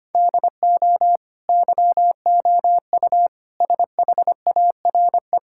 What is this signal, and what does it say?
Morse code, heard as a keyed tone, about 25 wpm, 700 Hz. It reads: DO YOU SHARE
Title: Sentences from Top 1600 Words 25wpm